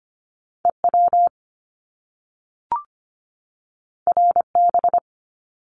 Morse code, heard as a keyed tone, about 25 wpm, 700 Hz. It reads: E W R B
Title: Top Two Letter Combinations in Words 25wpm